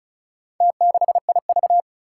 Morse code, heard as a keyed tone, about 36 wpm, 700 Hz. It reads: T6IV
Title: Four Challenging Letter-Number Combinations 35wpm